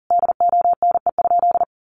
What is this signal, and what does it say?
Morse code, heard as a keyed tone, about 40 wpm, 700 Hz. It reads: DONE?
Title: Sentences from Top 1100 Words 40wpm